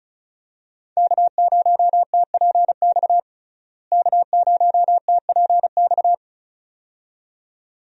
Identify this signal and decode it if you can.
Morse code, heard as a keyed tone, about 36 wpm, 700 Hz. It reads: K0TPX K0TPX
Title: USA Call Signs - 5 Characters - Repeated 2x - 35wpm